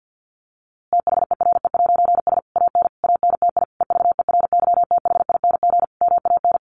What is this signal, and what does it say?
Morse code, heard as a keyed tone, about 50 wpm, 700 Hz. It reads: THERE'S AN ANTI EVERYTHING MAN
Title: A Sound Of Thunder 50wpm